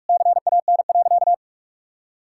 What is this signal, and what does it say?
Morse code, heard as a keyed tone, about 46 wpm, 700 Hz. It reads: CAN.
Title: Sentences from Top 1000 Words - Review Part 5 - 45wpm